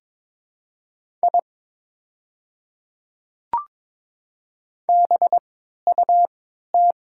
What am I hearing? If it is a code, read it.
Morse code, heard as a keyed tone, about 23 wpm, spaced as for 13 wpm, 700 Hz. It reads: I BUT
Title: 3x Character Spacing - Top 100 Words - 22wpm